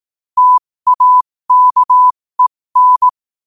Morse code, heard as a keyed tone, about 18 wpm, spaced as for 15 wpm, 1000 Hz. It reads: TAKEN
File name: qotdfiles / 15wpm